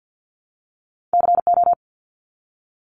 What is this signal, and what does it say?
Morse code, heard as a keyed tone, about 48 wpm, 700 Hz. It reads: CO
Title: US States Repeated 3x 50wpm